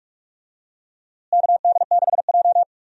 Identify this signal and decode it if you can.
Morse code, heard as a keyed tone, about 46 wpm, 700 Hz. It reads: KD6J
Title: USA Call Signs - 4 Characters - Repeated 3x - 45wpm